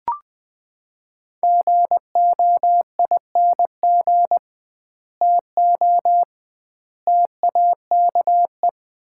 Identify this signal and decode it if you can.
Morse code, heard as a keyed tone, about 20 wpm, 700 Hz. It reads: GOING TO TAKE
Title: 2x Word-Spacing - Sets of 3 Words - 20wpm